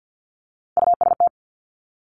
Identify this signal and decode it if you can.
Morse code, heard as a keyed tone, about 50 wpm, 700 Hz. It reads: UST
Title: Top Three Letter Combinations in Words 50wpm